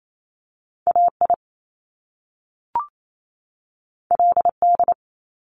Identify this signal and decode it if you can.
Morse code, heard as a keyed tone, about 28 wpm, 700 Hz. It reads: A I L D
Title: Top Two Letter Combinations in Words 28wpm